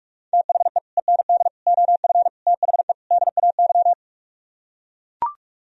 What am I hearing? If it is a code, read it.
Morse code, heard as a keyed tone, about 44 wpm, 700 Hz. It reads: THE END OF THE DAY